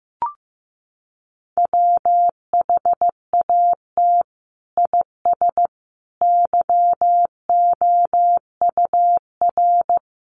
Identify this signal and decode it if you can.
Morse code, heard as a keyed tone, about 15 wpm, 700 Hz. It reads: WHAT IS YOUR
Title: Sentences from Top 600 Words 15wpm